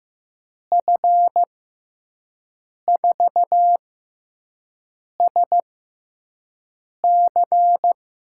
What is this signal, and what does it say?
Morse code, heard as a keyed tone, about 16 wpm, spaced as for 5 wpm, 700 Hz. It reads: F4SC
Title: USA Call Signs - 6 Characters - 6x Character Spacing - 15wpm